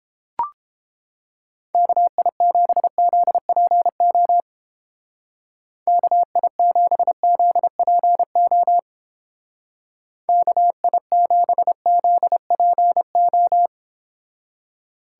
Morse code, heard as a keyed tone, about 29 wpm, 700 Hz. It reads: KI7ZPO KI7ZPO KI7ZPO